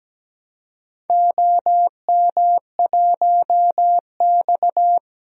Morse code, heard as a keyed tone, about 17 wpm, 700 Hz. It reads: OM1X
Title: DXCC-100 Call Signs - 4 Characters - 17wpm